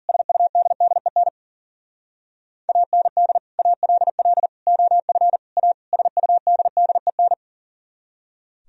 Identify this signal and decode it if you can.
Morse code, heard as a keyed tone, about 52 wpm, 700 Hz. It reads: SUDDEN AND ALL OF A SUDDEN